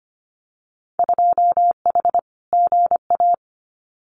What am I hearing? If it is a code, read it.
Morse code, heard as a keyed tone, about 25 wpm, 700 Hz. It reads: 2H GA
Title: Winter Field Day Exchanges 25wpm